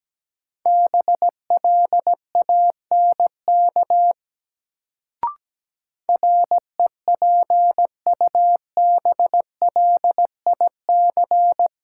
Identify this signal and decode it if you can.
Morse code, heard as a keyed tone, about 17 wpm, 700 Hz. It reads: BLANK REPUBLIC
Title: Sentences from Top 1500 Words - New Word Drill Alt - 17wpm